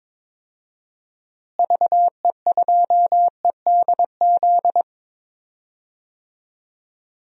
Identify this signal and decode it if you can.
Morse code, heard as a keyed tone, about 22 wpm, 700 Hz. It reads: VE2EDZ
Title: DXCC-100 Call Signs - 6 Characters - Repeated 3x - 22wpm